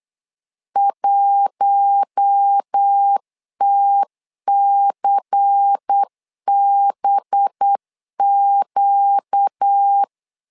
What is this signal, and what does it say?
Morse code, heard as a keyed tone, about 8 wpm, 800 Hz. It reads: 1TCBQ
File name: LEZIONI / 5 LBVC / 5 LBVC 8wpm 2935